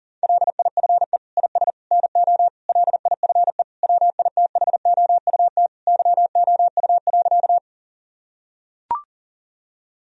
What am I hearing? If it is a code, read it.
Morse code, heard as a keyed tone, about 40 wpm, 700 Hz. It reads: LIFE IS NO LIFE WITHOUT YOU.
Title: Sentences from Top 1000 Words - Review Part 3 - 40wpm